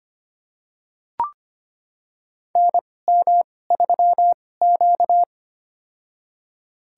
Morse code, heard as a keyed tone, about 25 wpm, spaced as for 18 wpm, 700 Hz. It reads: NM3Q
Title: USA Call Signs - 4 Characters - 2x Character Spacing - 25wpm